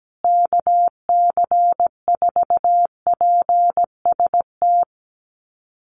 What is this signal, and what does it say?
Morse code, heard as a keyed tone, about 17 wpm, 700 Hz. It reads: KC4PST
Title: USA Call Signs Repeated 3x 17wpm